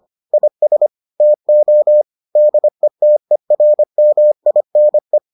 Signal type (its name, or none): Morse code